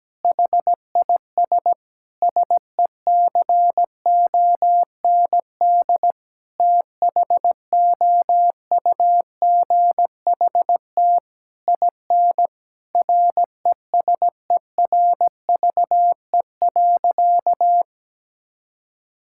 Morse code, heard as a keyed tone, about 17 wpm, 700 Hz. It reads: HIS SECOND THOUGHT IN RESERVE.
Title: Sentences from Top 1000 Words - Review Part 6 - 17wpm